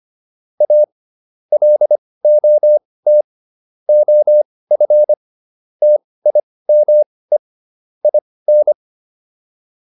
Morse code, heard as a keyed tone, about 25 wpm, spaced as for 18 wpm, 600 Hz. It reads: A LOT OF TIME IN